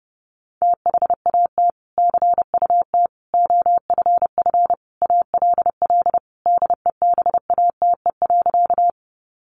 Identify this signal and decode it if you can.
Morse code, heard as a keyed tone, about 30 wpm, 700 Hz. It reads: THAT CUT OFF ALL DEBATE.
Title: Sentences from Top 1100 Words 30wpm